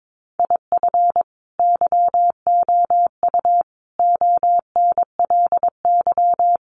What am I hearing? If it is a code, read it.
Morse code, heard as a keyed tone, about 22 wpm, 700 Hz. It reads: IF YOU ONLY
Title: Sentences from Top 400 Words 22wpm